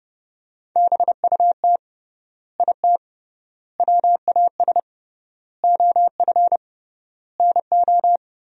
Morse code, heard as a keyed tone, about 30 wpm, 700 Hz. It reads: BUT IT WAS OF NO
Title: 3x Word-Spacing - Sentences from Top 100 Words - 30wpm